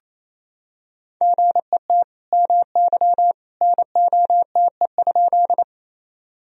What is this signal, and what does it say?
Morse code, heard as a keyed tone, about 28 wpm, 700 Hz. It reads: GET MY NOTE?
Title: Sentences from Top 1000 Words - Review Part 2 - 28wpm